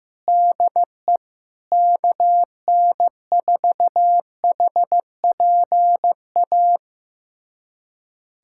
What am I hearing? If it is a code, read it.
Morse code, heard as a keyed tone, about 15 wpm, 700 Hz. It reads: DE KN4HPA
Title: USA Callsign DE Callsign - 6 Characters - 15wpm